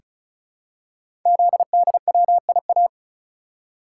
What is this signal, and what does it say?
Morse code, heard as a keyed tone, about 36 wpm, 690 Hz. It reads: ZDWIA
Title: Code Groups - 5 Letters - 35wpm